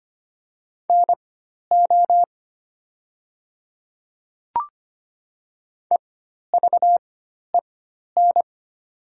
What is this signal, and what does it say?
Morse code, heard as a keyed tone, about 25 wpm, spaced as for 12 wpm, 700 Hz. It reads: NO EVEN
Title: 4x Character Spacing - Top 200 Words - 25wpm